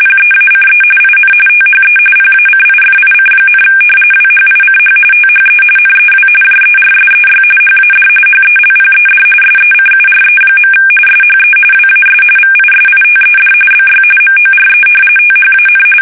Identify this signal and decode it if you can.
RTTY, about 45.45 baud. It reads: PEACE IN THE PACIFIC, WHICH THE NEW JERSEY HAD HELPED SECURE. SHE WAS VISITED BY MORE THAN
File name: fsk850 / battleshipnj